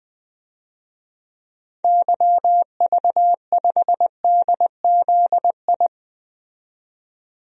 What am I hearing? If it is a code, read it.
Morse code, heard as a keyed tone, about 20 wpm, 700 Hz. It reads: YV5DZI
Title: DXCC-100 Call Signs - 6 Characters - 20wpm